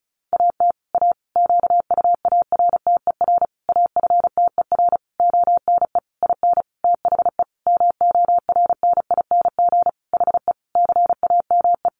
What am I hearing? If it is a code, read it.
Morse code, heard as a keyed tone, about 34 wpm, 700 Hz. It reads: AT A QUARTER AFTER ONE IN THE MORNING HE CAME
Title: Sentences from Top 1300 Words 35wpm